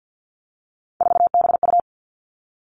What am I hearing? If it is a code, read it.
Morse code, heard as a keyed tone, about 50 wpm, 700 Hz. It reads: 4BU
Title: Three Challenging Letter-Number Combinations 50wpm